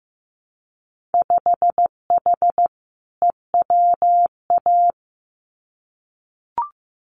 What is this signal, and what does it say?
Morse code, heard as a keyed tone, about 15 wpm, 700 Hz. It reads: 5H EWA
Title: Winter Field Day Exchanges Encore 15wpm